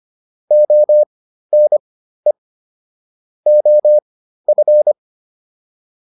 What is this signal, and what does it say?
Morse code, heard as a keyed tone, about 25 wpm, spaced as for 13 wpm, 600 Hz. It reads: ONE OF
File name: WD302 13